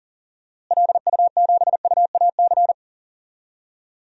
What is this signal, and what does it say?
Morse code, heard as a keyed tone, about 40 wpm, 700 Hz. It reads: LU7UAC
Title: DXCC-100 Call Signs - 6 Characters - Repeated 3x - 40wpm